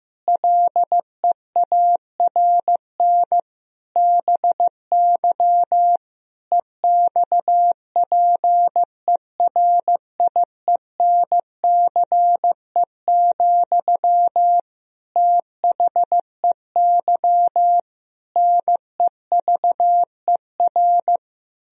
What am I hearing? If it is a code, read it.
Morse code, heard as a keyed tone, about 15 wpm, 700 Hz. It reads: LEARN BY EXPERIENCE, THEY NEVER
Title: Sentences from Top 1000 Words - Review Part 3 - 15wpm